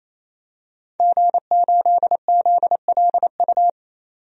Morse code, heard as a keyed tone, about 27 wpm, 700 Hz. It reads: G8ZLU
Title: DXCC-100 Call Signs - 5 Characters - 28wpm